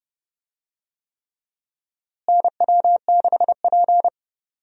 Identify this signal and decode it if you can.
Morse code, heard as a keyed tone, about 30 wpm, 700 Hz. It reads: NW6P
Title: USA Call Signs - 4 Characters - Repeated 3x - 30wpm